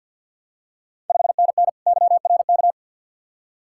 Morse code, heard as a keyed tone, about 51 wpm, 700 Hz. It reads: 5NN YRK
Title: Ontario QSO Party Exchange 50wpm